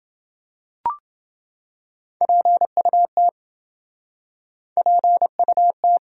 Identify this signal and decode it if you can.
Morse code, heard as a keyed tone, about 28 wpm, 700 Hz. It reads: PUT PUT